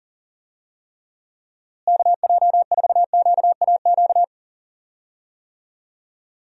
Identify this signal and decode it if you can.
Morse code, heard as a keyed tone, about 39 wpm, 700 Hz. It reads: KJ4QAQ
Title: USA Call Signs - 6 Characters - Repeated 2x - 40wpm